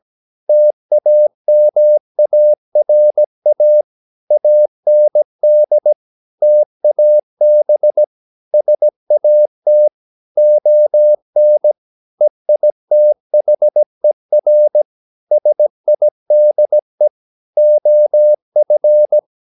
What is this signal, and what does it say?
Morse code, heard as a keyed tone, about 17 wpm, 600 Hz. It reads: TAMARA AND TAB SAT ON EITHER SIDE OF
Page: Romanen Never af Ken Follet på engelsk.